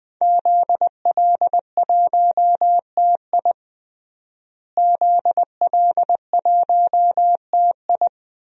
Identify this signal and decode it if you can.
Morse code, heard as a keyed tone, about 20 wpm, 700 Hz. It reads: ZL1TI ZL1TI
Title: DXCC-100 Call Signs - 5 Characters - Repeated 2x - 20wpm